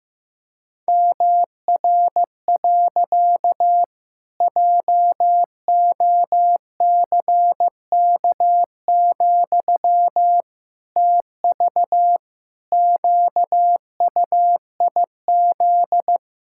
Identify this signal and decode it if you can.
Morse code, heard as a keyed tone, about 15 wpm, 700 Hz. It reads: MR. JOCK, TV QUIZ
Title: Pangram Challenge v2 - 15wpm